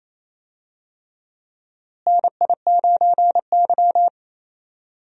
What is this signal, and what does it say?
Morse code, heard as a keyed tone, about 28 wpm, 700 Hz. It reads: NI9Y
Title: USA Call Signs - 4 Characters - Repeated 3x - 28wpm